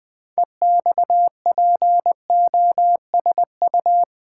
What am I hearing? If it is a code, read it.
Morse code, heard as a keyed tone, about 20 wpm, 700 Hz. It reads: EXPOSU
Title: Build A Word - 3 Letters At A Time - 8-Letter Words - 20wpm